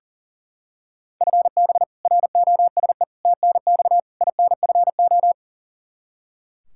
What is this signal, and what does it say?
Morse code, heard as a keyed tone, about 40 wpm, 700 Hz. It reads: FB ROSE TNX INFO
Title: Practice QSO - Exchange 3 - More Info - 40wpm